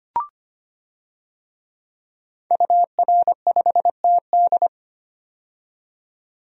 Morse code, heard as a keyed tone, about 25 wpm, 700 Hz. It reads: UR5TD
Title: DXCC-100 Call Signs - 5 Characters - 25wpm